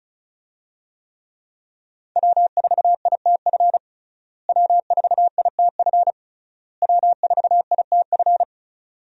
Morse code, heard as a keyed tone, about 34 wpm, 700 Hz. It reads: W4ITF W4ITF W4ITF
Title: USA Call Signs - 5 Characters - Repeated 3x - 35wpm